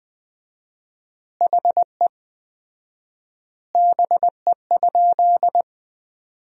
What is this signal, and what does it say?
Morse code, heard as a keyed tone, about 20 wpm, 700 Hz. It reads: HE BE?
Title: 4x Word-Spacing - Sentences from Top 200 Words - 20wpm